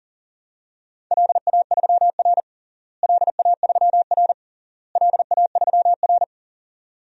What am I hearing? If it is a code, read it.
Morse code, heard as a keyed tone, about 40 wpm, 700 Hz. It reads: LA3R LA3R LA3R
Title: DXCC-100 Call Signs - 4 Characters - Repeated 3x - 40wpm